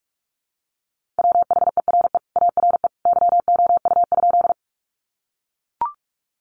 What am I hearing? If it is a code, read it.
Morse code, heard as a keyed tone, about 44 wpm, 700 Hz. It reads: WHERE ARE YOU?
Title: Sentences from Top 200 Words 45wpm